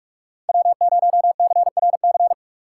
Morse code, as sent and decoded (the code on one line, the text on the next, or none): .-- ----- -.-. .-. -.-.
W0CRC